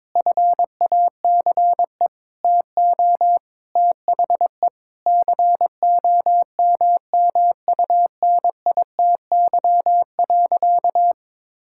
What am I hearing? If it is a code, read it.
Morse code, heard as a keyed tone, about 21 wpm, 700 Hz. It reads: FACE TO THE COMMUNITY.
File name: practice / 22wpm / Sentences with 8-Word Phrases - 22wpm